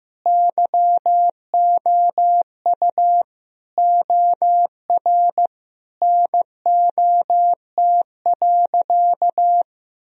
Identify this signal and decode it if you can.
Morse code, heard as a keyed tone, about 15 wpm, 700 Hz. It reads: YOU OR NOT.
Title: Sentences from Top 1000 Words - Review Part 3 - 15wpm